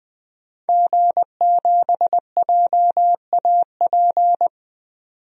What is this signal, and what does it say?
Morse code, heard as a keyed tone, about 20 wpm, 700 Hz. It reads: G7JAP
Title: DXCC-100 Call Signs - 5 Characters - 20wpm